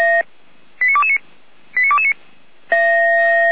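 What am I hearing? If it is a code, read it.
Five-tone selective call (ZVEI), heard as quick runs of five tones, tones of about 70 ms. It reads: 89229 89229
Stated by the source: ZVEI five-tone fire brigade call received on SDR.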